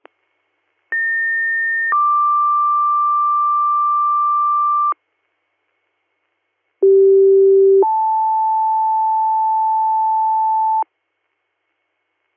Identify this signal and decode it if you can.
Two-tone page, heard as a one-second tone, then a three-second tone, about 1800 then 1180 Hz, about 390 then 880 Hz. It is heard from 0.9 to 10.8 s.